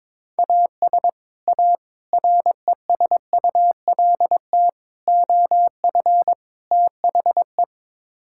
Morse code, heard as a keyed tone, about 22 wpm, 700 Hz. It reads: AS A RESULT OF THE
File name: practice / 22wpm / Build A Phrase - 5-Word Phrases - 22wpm